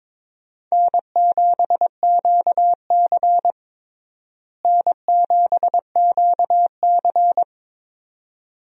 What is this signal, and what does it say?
Morse code, heard as a keyed tone, about 22 wpm, 700 Hz. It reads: N7QC N7QC
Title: USA Call Signs - 4 Characters - Repeated 2x - 22wpm